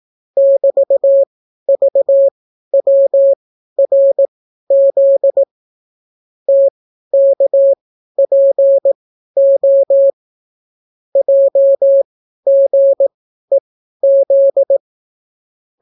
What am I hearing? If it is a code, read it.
Morse code, heard as a keyed tone, about 18 wpm, spaced as for 12 wpm, 550 Hz. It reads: <BT>VWRZ TKPO JGEZ